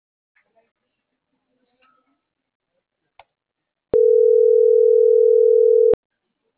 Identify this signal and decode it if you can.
Ringback tone (North America), 440 + 480 Hz.